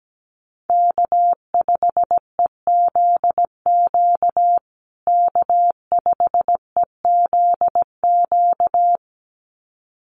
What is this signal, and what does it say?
Morse code, heard as a keyed tone, about 17 wpm, 700 Hz. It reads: K5EZQ K5EZQ
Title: USA Call Signs Repeated 2x 17wpm